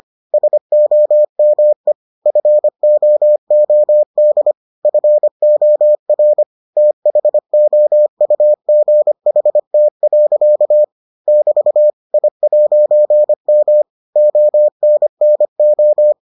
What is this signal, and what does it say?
Morse code, heard as a keyed tone, about 25 wpm, 600 Hz. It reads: SOME FOOD FOR THOUGHT. <BT> I'M ONNO